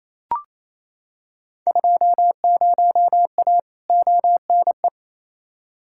Morse code, heard as a keyed tone, about 28 wpm, 700 Hz. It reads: 20A ONE
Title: Field Day Exchanges Encore - 28wpm